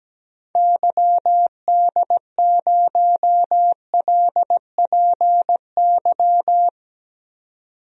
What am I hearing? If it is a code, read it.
Morse code, heard as a keyed tone, about 17 wpm, 700 Hz. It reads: YD0LPY
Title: DXCC-100 Call Signs - 6 Characters - 17wpm